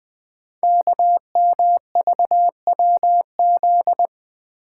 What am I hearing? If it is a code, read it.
Morse code, heard as a keyed tone, about 20 wpm, 700 Hz. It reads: KMVWZ
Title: Code Groups - 5 Letters - 20wpm